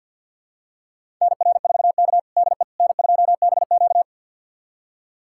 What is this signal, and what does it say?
Morse code, heard as a keyed tone, about 52 wpm, 700 Hz. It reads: NR4K DE N2BQ
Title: USA Callsign DE Callsign - 4 Characters - 50wpm